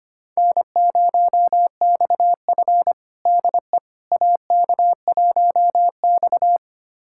Morse code, heard as a keyed tone, about 25 wpm, 700 Hz. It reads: N0XF DE AK1X
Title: USA Callsign DE Callsign - 4 Characters - 25wpm